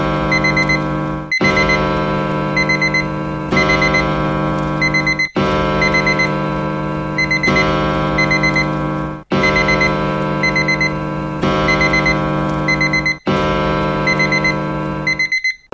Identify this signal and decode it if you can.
There is no code.